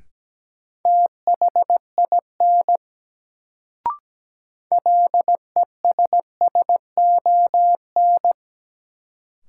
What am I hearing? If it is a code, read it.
Morse code, heard as a keyed tone, about 17 wpm, 700 Hz. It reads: THIN LESSON
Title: Sentences from Top 1200 Words - New Word Drill - 17wpm